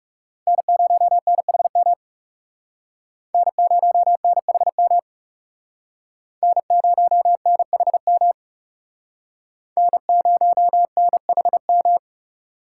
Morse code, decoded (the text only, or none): N0NHM N0NHM N0NHM N0NHM